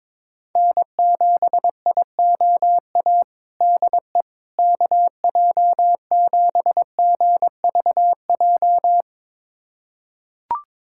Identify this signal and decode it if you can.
Morse code, heard as a keyed tone, about 22 wpm, 700 Hz. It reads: N7IOA DE KJ7GVJ
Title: USA Callsign DE Callsign 22wpm